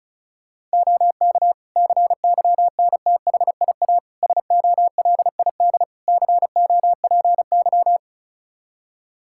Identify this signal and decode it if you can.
Morse code, heard as a keyed tone, about 34 wpm, 700 Hz. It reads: OK CYNTHIA SOLID COPY
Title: Practice QSO - Exchange 3 - More Info - 35wpm